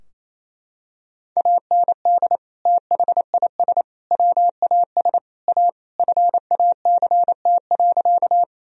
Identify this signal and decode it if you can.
Morse code, heard as a keyed tone, about 27 wpm, 700 Hz. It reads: AND THIS WAS A FACT.
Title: Sentences from Top 1000 Words - Review Part 3 - 28wpm